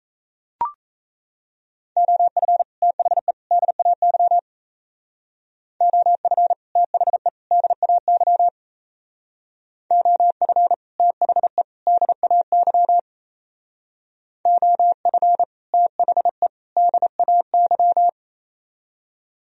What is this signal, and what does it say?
Morse code, heard as a keyed tone, about 36 wpm, 700 Hz. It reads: OF THE DAY OF THE DAY OF THE DAY OF THE DAY